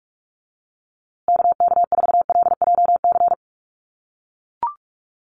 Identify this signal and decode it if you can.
Morse code, heard as a keyed tone, about 44 wpm, 700 Hz. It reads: KK4LJC
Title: USA Call Signs Repeated 3x 45wpm